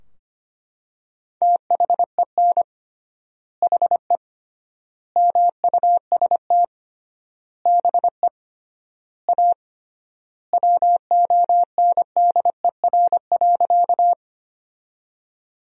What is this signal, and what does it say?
Morse code, heard as a keyed tone, about 25 wpm, 700 Hz. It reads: THEN HE MUST BE A WONDER.
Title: 3x Word-Spacing - Sentences from Top 500 Words - 25wpm